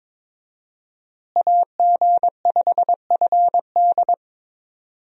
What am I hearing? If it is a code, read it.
Morse code, heard as a keyed tone, about 22 wpm, 700 Hz. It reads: AG5FD
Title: USA Call Signs - 5 Characters - Repeated 2x - 22wpm